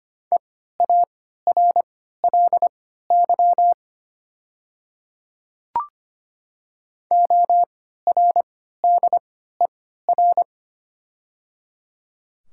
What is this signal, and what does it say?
Morse code, heard as a keyed tone, about 25 wpm, spaced as for 14 wpm, 690 Hz. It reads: EARLY ORDER
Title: 3x Character Spacing - Top 500 Words - 25wpm